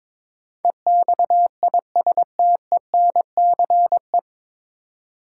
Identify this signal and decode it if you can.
Morse code, heard as a keyed tone, about 22 wpm, 700 Hz. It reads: EXISTENCE
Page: Morse Code Ninja practice 22wpm